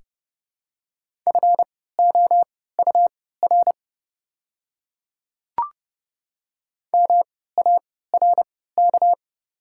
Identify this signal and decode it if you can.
Morse code, heard as a keyed tone, about 30 wpm, spaced as for 17 wpm, 700 Hz. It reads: FOUR MARK
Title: 3x Character Spacing - Top 300 Words - 30wpm